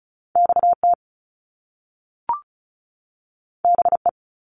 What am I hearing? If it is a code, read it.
Morse code, heard as a keyed tone, about 36 wpm, 700 Hz. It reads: XT BE
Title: Top Two Letter Combinations in Words 35wpm